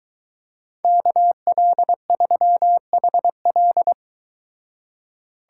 Morse code, heard as a keyed tone, about 23 wpm, 700 Hz. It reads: KL3HL